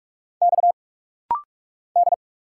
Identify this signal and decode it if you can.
Morse code, heard as a keyed tone, about 44 wpm, 700 Hz. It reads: X D
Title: Single Letter-Number Alt - Rapid-Fire 45wpm